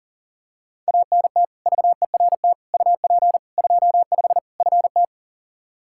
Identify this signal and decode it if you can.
Morse code, heard as a keyed tone, about 40 wpm, 700 Hz. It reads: ANT VERT UP 25 FT
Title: K1USN SST QSO Practice 40wpm